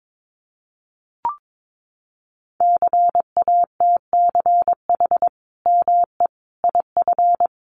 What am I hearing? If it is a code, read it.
Morse code, heard as a keyed tone, about 22 wpm, 700 Hz. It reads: CATCH ME IF
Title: Sentences from Top 1500 Words 22wpm